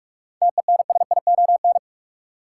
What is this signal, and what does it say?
Morse code, heard as a keyed tone, about 46 wpm, 700 Hz. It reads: TENSION
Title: Sentences from Top 1600 Words - New Word Drill - 45wpm